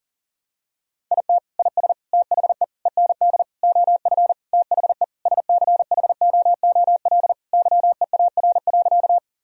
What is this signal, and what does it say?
Morse code, heard as a keyed tone, about 40 wpm, 700 Hz. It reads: IT IS THE END OF THE SCHOOL YEAR.
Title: Sentences from Top 1000 Words - Review Part 1 - 40wpm